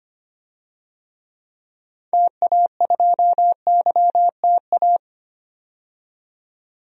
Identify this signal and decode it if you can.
Morse code, heard as a keyed tone, about 25 wpm, 700 Hz. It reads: TA2YTA
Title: DXCC-100 Call Signs - 6 Characters - 25wpm